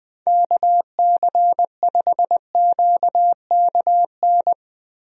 Morse code, heard as a keyed tone, about 21 wpm, 700 Hz. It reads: KC5QKN
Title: USA Call Signs - 6 Characters - Repeated 3x - 20wpm